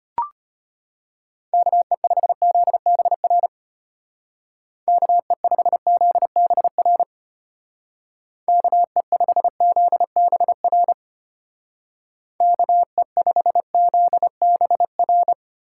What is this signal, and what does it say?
Morse code, heard as a keyed tone, about 30 wpm, 700 Hz. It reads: KE5ZBR KE5ZBR KE5ZBR KE5ZBR
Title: Speed Racing - USA 2x3 Call Signs 25wpm